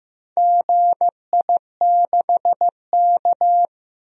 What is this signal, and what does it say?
Morse code, heard as a keyed tone, about 15 wpm, 700 Hz. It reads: GI6K
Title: DXCC-100 Call Signs - 4 Characters - Repeated 2x - 15wpm